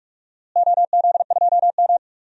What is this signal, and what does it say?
Morse code, heard as a keyed tone, about 45 wpm, 700 Hz. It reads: OZ2M
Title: DXCC-100 Call Signs - 4 Characters - Repeated 2x - 45wpm